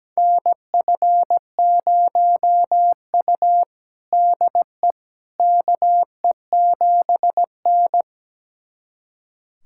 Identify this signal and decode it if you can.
Morse code, heard as a keyed tone, about 17 wpm, 700 Hz. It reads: NF0U DE KE7N
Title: USA Callsign DE Callsign - 4 Characters - 17wpm